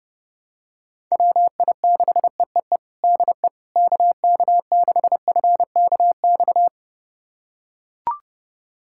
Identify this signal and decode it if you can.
Morse code, heard as a keyed tone, about 32 wpm, 700 Hz. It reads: WI6EEE DE KK6FKX
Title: USA Callsign DE Callsign - 6 Characters - 30wpm